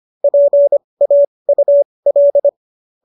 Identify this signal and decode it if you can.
Morse code, heard as a keyed tone, about 25 wpm, spaced as for 20 wpm, 590 Hz. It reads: PAUL